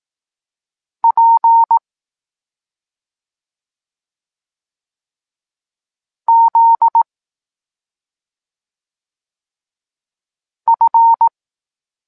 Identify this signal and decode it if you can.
Morse code, heard as a keyed tone, about 18 wpm, 920 Hz. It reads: P Z F